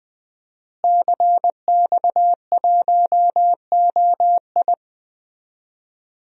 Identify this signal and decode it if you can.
Morse code, heard as a keyed tone, about 20 wpm, 700 Hz. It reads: CX1OI
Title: DXCC-100 Call Signs - 5 Characters - Repeated 2x - 20wpm